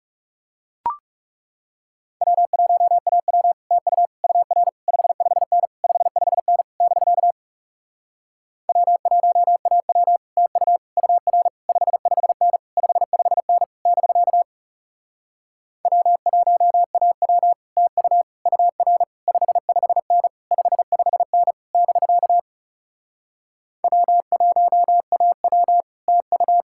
Morse code, heard as a keyed tone, about 40 wpm, 700 Hz. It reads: W1AW TU UR 55N 55N <BK> W1AW TU UR 55N 55N <BK> W1AW TU UR 55N 55N <BK> W1AW TU